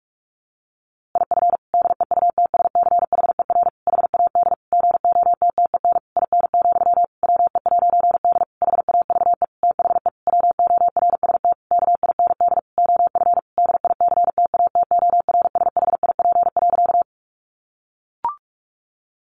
Morse code, heard as a keyed tone, about 44 wpm, 700 Hz. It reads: IF DEUTSCHER HAD GOTTEN IN, WE'D HAVE THE WORST KIND OF DICTATORSHIP.
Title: A Sound Of Thunder 45wpm